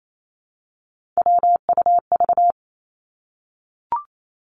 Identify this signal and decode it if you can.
Morse code, heard as a keyed tone, about 28 wpm, 700 Hz. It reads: WUV
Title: Three Challenging Letter-Number Combinations 28wpm